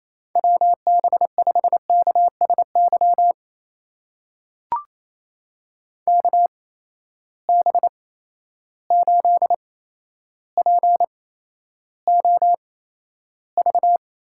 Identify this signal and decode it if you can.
Morse code, heard as a keyed tone, about 28 wpm, spaced as for 12 wpm, 700 Hz. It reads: WB5KSY KB8POV
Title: USA Call Signs - 6 Characters - 8x Character Spacing - 28wpm